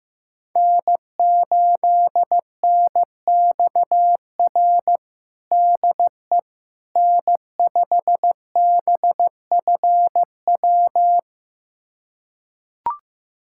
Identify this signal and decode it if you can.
Morse code, heard as a keyed tone, about 15 wpm, 700 Hz. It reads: N8NXR DE N5BFW
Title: USA Callsign DE Callsign - 5 Characters - 15wpm